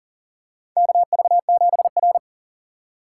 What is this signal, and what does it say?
Morse code, heard as a keyed tone, about 42 wpm, 700 Hz. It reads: KV7R
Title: USA Call Signs - 4x Character Spacing - 40wpm